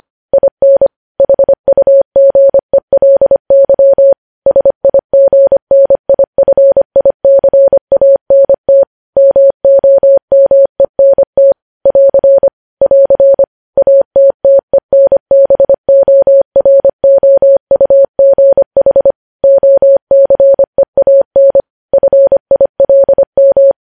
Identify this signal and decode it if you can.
Morse code, heard as a keyed tone, about 25 wpm, 550 Hz. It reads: IN HUGELY SIGNIFICANT MOMENT <AR> <AR> ATTENBOROUGH OCEAN FILM
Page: News Headlines in Morse Code at 25 WPM